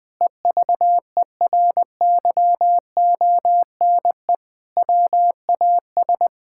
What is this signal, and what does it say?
Morse code, heard as a keyed tone, about 20 wpm, 700 Hz. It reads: EVERYONE WAS
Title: Sentences with 5-Word Phrases - 20wpm